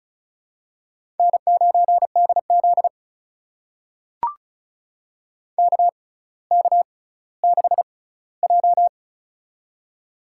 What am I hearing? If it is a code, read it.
Morse code, heard as a keyed tone, about 36 wpm, spaced as for 18 wpm, 700 Hz. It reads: N9DZ KK6J
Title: USA Call Signs - 6x Character Spacing - 35wpm